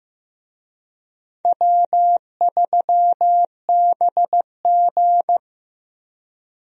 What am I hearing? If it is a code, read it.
Morse code, heard as a keyed tone, about 15 wpm, 700 Hz. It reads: W3BG
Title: USA Call Signs - 4 Characters - Repeated 2x - 15wpm